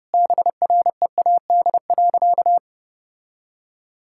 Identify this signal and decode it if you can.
Morse code, heard as a keyed tone, about 30 wpm, 700 Hz. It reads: BREAD.
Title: Sentences from Top 1600 Words 30wpm